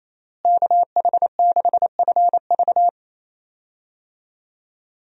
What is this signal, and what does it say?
Morse code, heard as a keyed tone, about 28 wpm, 700 Hz. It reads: KH6FV
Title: USA Call Signs - 5 Characters - Repeated 2x - 28wpm